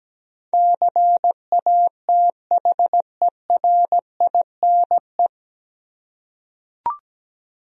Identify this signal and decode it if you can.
Morse code, heard as a keyed tone, about 17 wpm, 700 Hz. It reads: CATHERINE
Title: USA Common Names - 17wpm